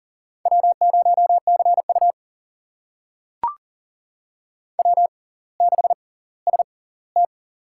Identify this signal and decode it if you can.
Morse code, heard as a keyed tone, about 40 wpm, spaced as for 20 wpm, 700 Hz. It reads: W0CU W6ST
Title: USA Call Signs - 4 Characters - 6x Character Spacing - 40wpm